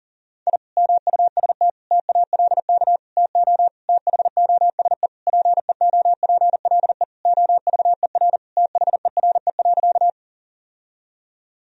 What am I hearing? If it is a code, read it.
Morse code, heard as a keyed tone, about 40 wpm, 700 Hz. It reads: I MUST TALK TO THOSE PEOPLE OVER THERE.
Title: Sentences from Top 1000 Words - Review Part 1 - 40wpm